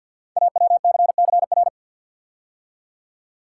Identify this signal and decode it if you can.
Morse code, heard as a keyed tone, about 47 wpm, 700 Hz. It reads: AWCCR
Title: Code Groups - 5 Letters - 50wpm